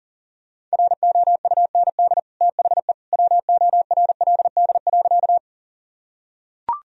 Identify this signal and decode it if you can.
Morse code, heard as a keyed tone, about 40 wpm, 700 Hz. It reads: ROUND THE WORLD.
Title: Sentences from Top 1000 Words - Review Part 1 - 40wpm